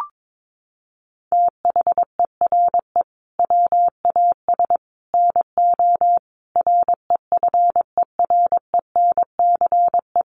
Morse code, heard as a keyed tone, about 22 wpm, 700 Hz. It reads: THERE WAS NO REFERENCE
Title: Sentences from Top 1200 Words 22wpm